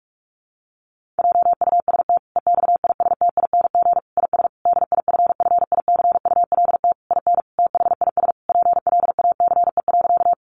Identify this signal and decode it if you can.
Morse code, heard as a keyed tone, about 44 wpm, 700 Hz. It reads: JUST EXISTING IS DIFFICULT IN THIS PLACE.
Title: Sentences from Top 1500 Words 45wpm